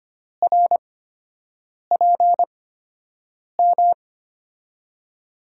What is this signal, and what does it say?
Morse code, heard as a keyed tone, about 25 wpm, spaced as for 7 wpm, 700 Hz. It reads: RPM
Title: USA Call Signs - 8x Character Spacing - 25wpm